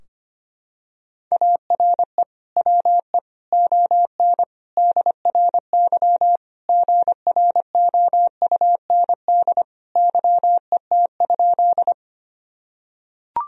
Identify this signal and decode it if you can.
Morse code, heard as a keyed tone, about 25 wpm, 700 Hz. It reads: ARE WE ON DRY GROUND YET?
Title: Sentences from Top 1000 Words - Review Part 3 - 25wpm